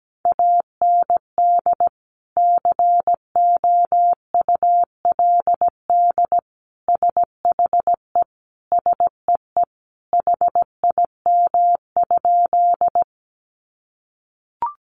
Morse code, heard as a keyed tone, about 17 wpm, 700 Hz. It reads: AND COULD SHE SEE HIM?
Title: Sentences from Top 100 Words 17wpm